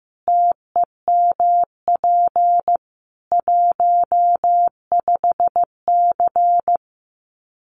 Morse code, heard as a keyed tone, about 15 wpm, 700 Hz. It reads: TEMP 15C
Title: Common QSO Phrases 15wpm